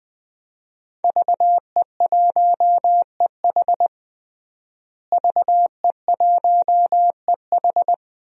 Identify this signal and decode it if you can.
Morse code, heard as a keyed tone, about 20 wpm, 700 Hz. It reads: VE1EH VE1EH
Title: DXCC-100 Call Signs - 5 Characters - Repeated 2x - 20wpm